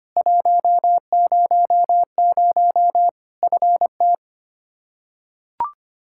Morse code, heard as a keyed tone, about 25 wpm, 700 Hz. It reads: 100 FT
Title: Practice QSO - Exchange 2 - Info - 25wpm